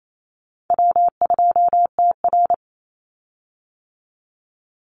Morse code, heard as a keyed tone, about 28 wpm, 700 Hz. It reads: W2TR
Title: Call Signs - 28wpm